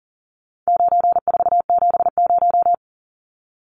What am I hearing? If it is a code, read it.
Morse code, heard as a keyed tone, about 40 wpm, 700 Hz. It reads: 9470
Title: Numbers - 4-Digit 40wpm